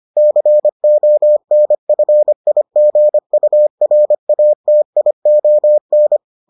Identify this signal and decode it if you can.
Morse code, heard as a keyed tone, about 25 wpm, 600 Hz. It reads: CONFIGURATION